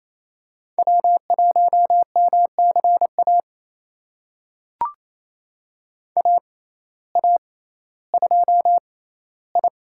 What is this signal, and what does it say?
Morse code, heard as a keyed tone, about 28 wpm, spaced as for 15 wpm, 700 Hz. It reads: W1MCA AA2I
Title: USA Call Signs - 5 Characters - 6x Character Spacing - 28wpm